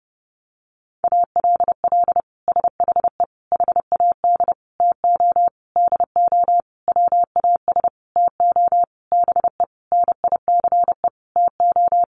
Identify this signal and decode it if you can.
Morse code, heard as a keyed tone, about 30 wpm, 700 Hz. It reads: ALL SHE HAD TO DO WAS TO BE NICE TO
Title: Sentences from Top 1100 Words 30wpm